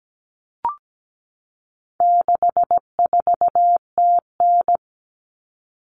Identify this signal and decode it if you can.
Morse code, heard as a keyed tone, about 17 wpm, 700 Hz. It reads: 64TN
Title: All Cut Numbers - 4-Digit 17wpm